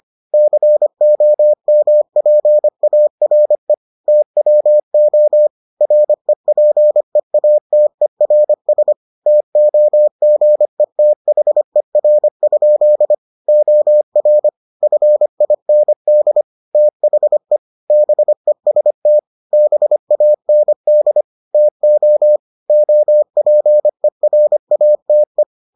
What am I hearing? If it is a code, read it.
Morse code, heard as a keyed tone, about 25 wpm, 600 Hz. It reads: COMPARE TWO REPEATERS TOGETHER? OR FIND THE BEST BAND TO OPERATE